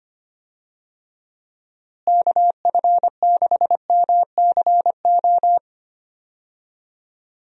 Morse code, heard as a keyed tone, about 25 wpm, 700 Hz. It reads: KF6MCO
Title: USA Call Signs - 6 Characters - Repeated 3x - 25wpm